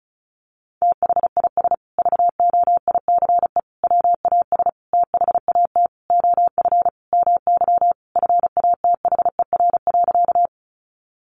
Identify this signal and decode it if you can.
Morse code, heard as a keyed tone, about 34 wpm, 700 Hz. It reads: THIS VOICE WAS THAT OF MY FATHER.
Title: Sentences from Top 500 Words 35wpm